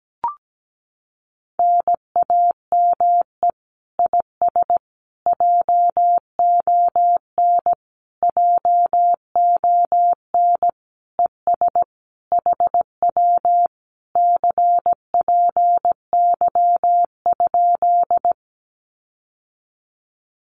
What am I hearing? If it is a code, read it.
Morse code, heard as a keyed tone, about 17 wpm, 700 Hz. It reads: NAME IS JON JON ES HW CPY?
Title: Common QSO Phrases Encore 17wpm